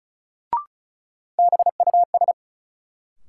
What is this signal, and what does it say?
Morse code, heard as a keyed tone, about 36 wpm, 700 Hz. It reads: BUS
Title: Sentences from Top 1000 Words - Review Part 6 - New Word Drill - 35wpm